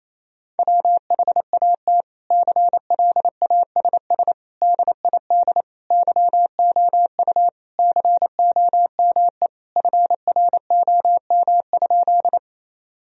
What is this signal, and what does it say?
Morse code, heard as a keyed tone, about 28 wpm, 700 Hz. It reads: WHAT CLASS DID YOU COME FROM?
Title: Sentences from Top 1000 Words - Review Part 2 - 28wpm